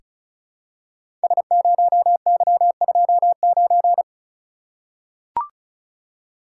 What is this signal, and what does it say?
Morse code, heard as a keyed tone, about 36 wpm, 700 Hz. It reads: S0Y29
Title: Code Groups - 5 Letter:Numbers - 35wpm